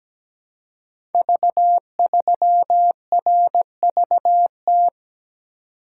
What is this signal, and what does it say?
Morse code, heard as a keyed tone, about 17 wpm, 700 Hz. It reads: V3RVT
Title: Code Groups - 5 Letter:Numbers - 17wpm